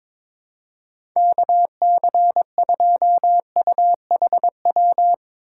Morse code, heard as a keyed tone, about 22 wpm, 700 Hz. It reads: KC2UHW
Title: USA Call Signs - 4x Character Spacing - 22wpm